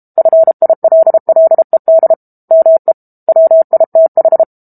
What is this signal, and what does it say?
Morse code, heard as a keyed tone, about 32 wpm, 650 Hz. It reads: FILLED ME WITH
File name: Book Two / 06-Chapter 6 - 32 WPM